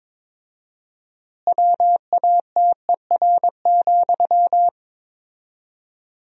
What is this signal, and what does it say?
Morse code, heard as a keyed tone, about 22 wpm, 700 Hz. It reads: WATER,
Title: 4x Word-Spacing - Sentences from Top 400 Words - 22wpm